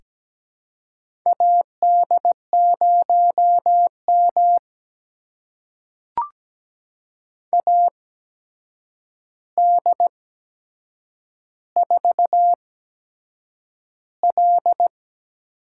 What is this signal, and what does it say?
Morse code, heard as a keyed tone, about 17 wpm, spaced as for 7 wpm, 700 Hz. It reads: AD0M AD4L
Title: USA Call Signs - 4 Characters - 8x Character Spacing - 17wpm